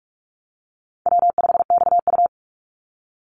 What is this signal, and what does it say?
Morse code, heard as a keyed tone, about 46 wpm, 700 Hz. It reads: W5XU
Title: Call Signs - 45wpm